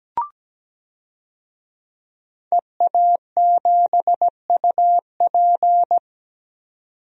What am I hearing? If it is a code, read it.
Morse code, heard as a keyed tone, about 17 wpm, 700 Hz. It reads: EA7UP
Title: DXCC-100 Call Signs - 5 Characters - 17wpm